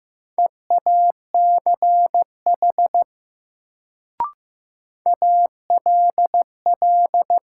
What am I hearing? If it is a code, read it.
Morse code, heard as a keyed tone, about 15 wpm, 700 Hz. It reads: EACH ALL
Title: Sets of 2 Words - New Word Drill - 15wpm